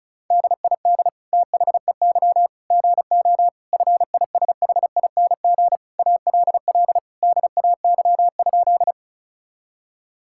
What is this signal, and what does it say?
Morse code, heard as a keyed tone, about 36 wpm, 700 Hz. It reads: DID THEY GO FISHING ALL DAY?
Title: Sentences from Top 1000 Words - Review Part 5 - 35wpm